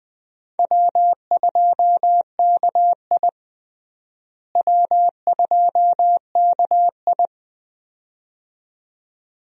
Morse code, heard as a keyed tone, about 20 wpm, 700 Hz. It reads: W2KI W2KI
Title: USA Call Signs - 4 Characters - Repeated 2x - 20wpm